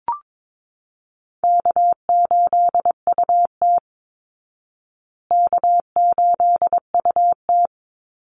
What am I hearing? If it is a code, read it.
Morse code, heard as a keyed tone, about 22 wpm, 690 Hz. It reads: K8UT K8UT
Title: Easy Call Signs Repeated 2x - 22wpm